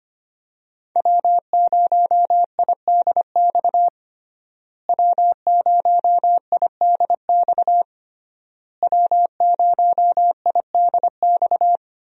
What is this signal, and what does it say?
Morse code, heard as a keyed tone, about 25 wpm, 700 Hz. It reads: W0IDX W0IDX W0IDX
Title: USA Call Signs - 5 Characters - Repeated 3x - 25wpm